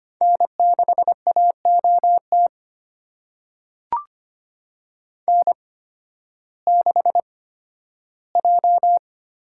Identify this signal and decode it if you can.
Morse code, heard as a keyed tone, about 25 wpm, spaced as for 13 wpm, 700 Hz. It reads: N6AOT N6J
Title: USA Call Signs - 5 Characters - 8x Character Spacing - 25wpm